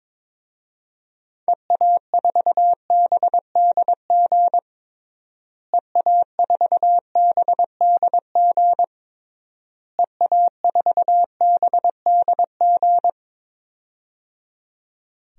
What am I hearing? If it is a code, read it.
Morse code, heard as a keyed tone, about 21 wpm, 700 Hz. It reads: EA4BDG EA4BDG EA4BDG
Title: DXCC-100 Call Signs - 6 Characters - Repeated 3x - 22wpm